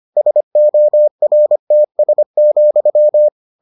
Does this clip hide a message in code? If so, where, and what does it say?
Yes: Morse code from 0.2 to 3.3 s, heard as a keyed tone, about 25 wpm, 600 Hz. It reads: SORTS,